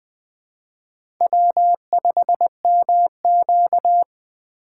Morse code, heard as a keyed tone, about 20 wpm, 700 Hz. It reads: W5MQ